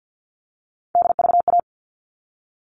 Morse code, heard as a keyed tone, about 50 wpm, 700 Hz. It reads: DVA
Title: All Cut Numbers - 3-Digit 50wpm